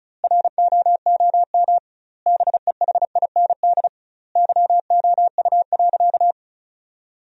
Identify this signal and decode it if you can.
Morse code, heard as a keyed tone, about 36 wpm, 700 Hz. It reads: ROOM BEHIND YOU.
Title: 2x Word-Spacing - Sentences from Top 500 Words - 35wpm